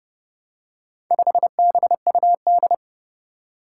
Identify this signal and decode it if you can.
Morse code, heard as a keyed tone, about 30 wpm, 700 Hz. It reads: 5BUD
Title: Four Challenging Letter-Number Combinations 30wpm